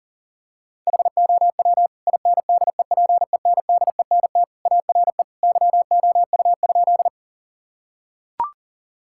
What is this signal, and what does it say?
Morse code, heard as a keyed tone, about 40 wpm, 700 Hz. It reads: HOW INDEPENDENT ARE YOU?
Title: Sentences from Top 1000 Words - Review Part 6 - 40wpm